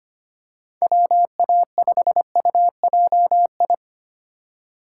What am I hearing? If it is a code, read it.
Morse code, heard as a keyed tone, about 25 wpm, 700 Hz. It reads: WA5UJI
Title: USA Call Signs - 3x Character Spacing - 25wpm